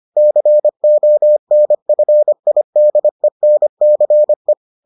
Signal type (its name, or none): Morse code